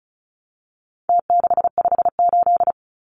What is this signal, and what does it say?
Morse code, heard as a keyed tone, about 34 wpm, 700 Hz. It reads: T658
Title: Numbers - Contest Style - 4-Digit 35wpm